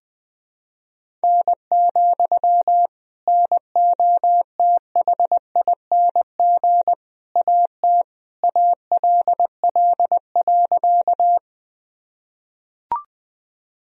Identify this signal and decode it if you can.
Morse code, heard as a keyed tone, about 20 wpm, 700 Hz. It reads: N, NOTHING AT ALL.
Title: Sentences from Top 500 Words and Abbreviations 20wpm